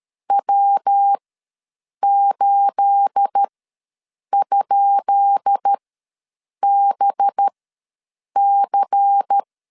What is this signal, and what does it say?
Morse code, heard as a keyed tone, about 13 wpm, spaced as for 7 wpm, 800 Hz. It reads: W8?BC
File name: LEZIONI / 10 SIGNS / 10B INT ws 12wpm 3527